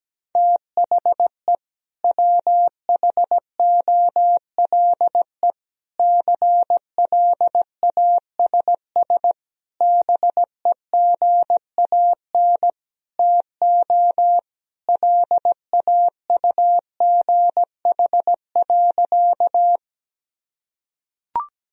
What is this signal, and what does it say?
Morse code, heard as a keyed tone, about 17 wpm, 700 Hz. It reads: THE WHOLE CLASS BEGAN TO LAUGH.
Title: Sentences from Top 1000 Words - Review Part 2 - 17wpm